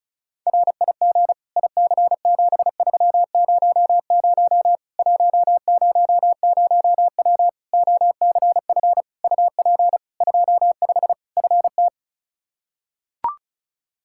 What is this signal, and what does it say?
Morse code, heard as a keyed tone, about 34 wpm, 700 Hz. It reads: RIG IC7300 100W OCF UP 25 FT
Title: Practice QSO - Exchange 2 - Info - 35wpm